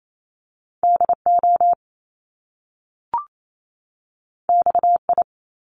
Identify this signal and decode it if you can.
Morse code, heard as a keyed tone, about 28 wpm, 700 Hz. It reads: DO XI
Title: Top Two Letter Combinations in Words 28wpm